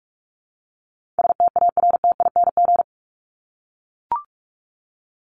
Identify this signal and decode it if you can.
Morse code, heard as a keyed tone, about 44 wpm, 700 Hz. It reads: STARTING
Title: ING Suffix - Top 100 Words with Base Words 45wpm